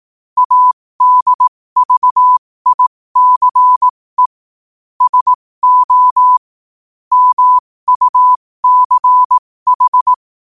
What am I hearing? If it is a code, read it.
Morse code, heard as a keyed tone, about 18 wpm, spaced as for 15 wpm, 1000 Hz. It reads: ADVICE SO MUCH
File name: qotdfiles / 15wpm